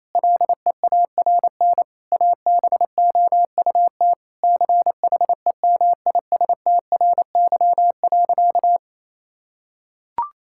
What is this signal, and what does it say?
Morse code, heard as a keyed tone, about 28 wpm, 700 Hz. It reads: LEARN ABOUT CHEMISTRY.
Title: Sentences from Top 1000 Words - Review Part 6 - 28wpm